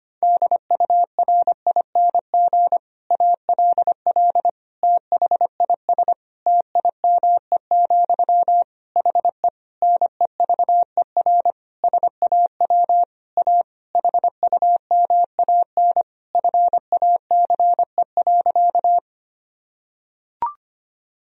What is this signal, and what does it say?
Morse code, heard as a keyed tone, about 25 wpm, 700 Hz. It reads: DURING ALL THIS TIME, HE NEVER SAW A HUMAN FACE.
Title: Sentences from Top 1000 Words - Review Part 3 - 25wpm